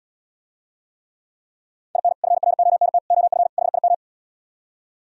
Morse code, heard as a keyed tone, about 77 wpm, 700 Hz. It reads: IN FRONT OF HIM